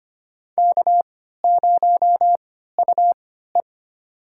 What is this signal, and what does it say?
Morse code, heard as a keyed tone, about 25 wpm, spaced as for 14 wpm, 700 Hz. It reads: K0UE
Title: USA Call Signs - 4 Characters - 3x Character Spacing - 25wpm